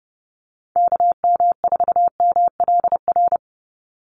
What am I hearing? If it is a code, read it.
Morse code, heard as a keyed tone, about 30 wpm, 700 Hz. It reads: KM4MLR
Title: USA Call Signs 30wpm